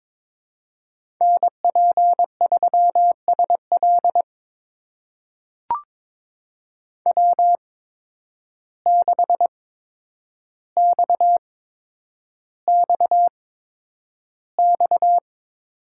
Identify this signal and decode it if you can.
Morse code, heard as a keyed tone, about 22 wpm, spaced as for 9 wpm, 700 Hz. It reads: NP3SL W6XXX
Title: USA Call Signs - 5 Characters - 8x Character Spacing - 22wpm